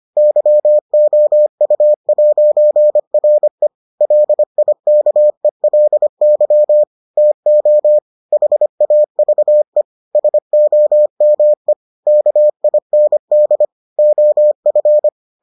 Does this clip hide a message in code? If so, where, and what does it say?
Yes: Morse code from 0.2 to 15.1 s, heard as a keyed tone, about 25 wpm, 600 Hz. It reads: YOU'RE LIKELY TO HAVE SOME KIND OF